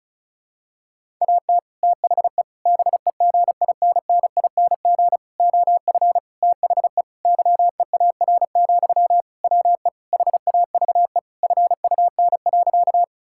Morse code, heard as a keyed tone, about 36 wpm, 700 Hz. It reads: AT THE BEGINNING OF THE YEAR, WE HAVE FUN.
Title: Sentences from Top 1000 Words - Review Part 5 - 35wpm